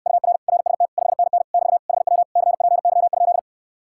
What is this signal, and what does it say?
Morse code, heard as a keyed tone, about 69 wpm, 700 Hz. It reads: UR RST 5NN <BT> HW CPY?